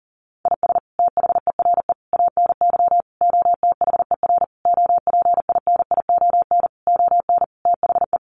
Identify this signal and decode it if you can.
Morse code, heard as a keyed tone, about 37 wpm, 700 Hz. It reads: IS THERE ANY OTHER OPINION ON THE
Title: Sentences from Top 1200 Words 40wpm